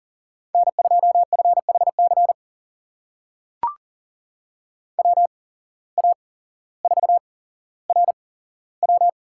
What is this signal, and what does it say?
Morse code, heard as a keyed tone, about 40 wpm, spaced as for 17 wpm, 700 Hz. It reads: N2FHC WA4RW
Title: USA Call Signs - 5 Characters - 8x Character Spacing - 40wpm